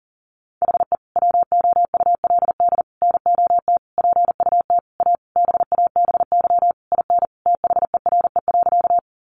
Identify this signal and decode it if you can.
Morse code, heard as a keyed tone, about 40 wpm, 700 Hz. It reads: HE WOULD NOT PUT A BABY IN THERE.
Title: Easy Phrases 40wpm